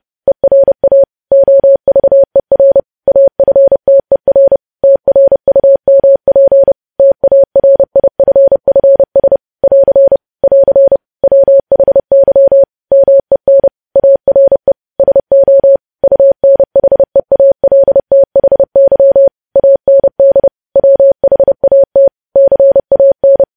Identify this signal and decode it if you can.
Morse code, heard as a keyed tone, about 30 wpm, 550 Hz. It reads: ERA OVER AFTER TRUMP TARIFFS <AR> <AR> WHY MEN ARE SO UNHEALTHY AND WHAT CAN
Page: news headlines in morse code at 30 wpm